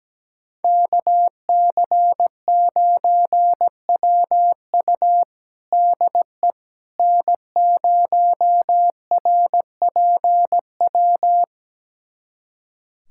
Morse code, heard as a keyed tone, about 17 wpm, 700 Hz. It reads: KC9WU DE N0RPW
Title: USA Callsign DE Callsign - 5 Characters - 17wpm